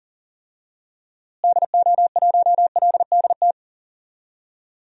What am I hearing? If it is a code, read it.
Morse code, heard as a keyed tone, about 40 wpm, 700 Hz. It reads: DO1LDT
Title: DXCC-100 Call Signs - 6 Characters - Repeated 2x - 40wpm